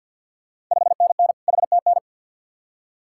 Morse code, heard as a keyed tone, about 52 wpm, 700 Hz. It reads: 5NN HTN
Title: Ontario QSO Party Exchange 50wpm